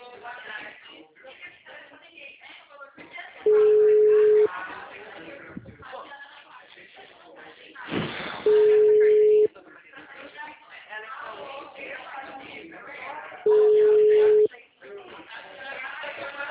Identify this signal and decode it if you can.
Ringback tone (Europe, 425 Hz), with speech.
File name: out- -2007- - - .69669